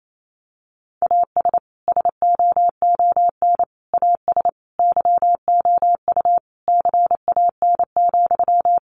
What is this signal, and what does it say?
Morse code, heard as a keyed tone, about 28 wpm, 700 Hz. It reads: AS SOON AS YOU CAN,
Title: Sentences from Top 1500 Words 28wpm